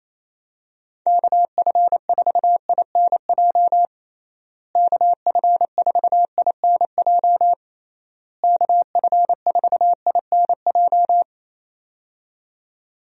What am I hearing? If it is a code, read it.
Morse code, heard as a keyed tone, about 28 wpm, 700 Hz. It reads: KF4INJ KF4INJ KF4INJ
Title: USA Call Signs - 6 Characters - Repeated 3x - 28wpm